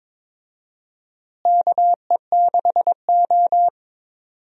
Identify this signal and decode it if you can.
Morse code, heard as a keyed tone, about 22 wpm, 700 Hz. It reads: KE6O
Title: USA Call Signs - 4 Characters - Repeated 3x - 22wpm